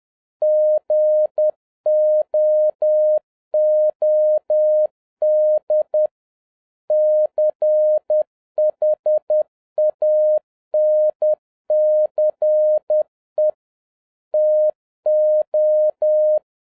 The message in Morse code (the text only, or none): GOOD CHANCE TO